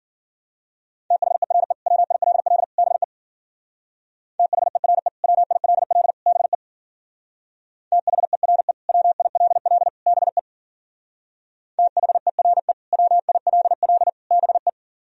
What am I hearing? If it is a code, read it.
Morse code, heard as a keyed tone, about 50 wpm, 700 Hz. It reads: THERE WILL BE THERE WILL BE THERE WILL BE THERE WILL BE
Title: Speed-Racing - Sets of 3 Words 40wpm